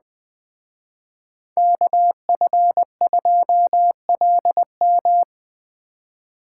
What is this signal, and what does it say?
Morse code, heard as a keyed tone, about 20 wpm, 700 Hz. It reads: KF2LM
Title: Speed Racing - USA 2x2 Call Signs 17wpm